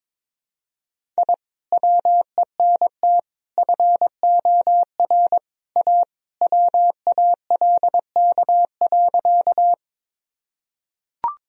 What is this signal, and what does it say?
Morse code, heard as a keyed tone, about 22 wpm, 700 Hz. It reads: I WENT FOR A WALK.
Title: Sentences from Top 1000 Words - Review Part 1 - 22wpm